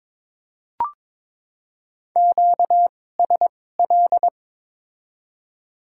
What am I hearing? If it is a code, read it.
Morse code, heard as a keyed tone, about 22 wpm, spaced as for 16 wpm, 700 Hz. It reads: QSL
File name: Top 90 QSO Elements - 2x Character Spacing - 22wpm